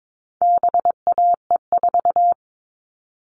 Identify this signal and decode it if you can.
Morse code, heard as a keyed tone, about 22 wpm, 700 Hz. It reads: BAE4
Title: All Cut Numbers - 4-Digit 22wpm